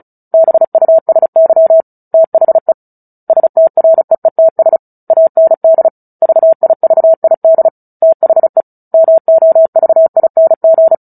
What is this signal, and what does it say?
Morse code, heard as a keyed tone, about 35 wpm, 650 Hz. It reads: BUSY THE STREETS AND VIVID THE MOVING
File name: Book Two / 09-Chapter 9 - 35 WPM